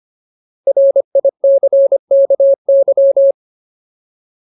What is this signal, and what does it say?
Morse code, heard as a keyed tone, about 25 wpm, 550 Hz. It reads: RICKY
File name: Ultimate Word List / 25WPM Ultimate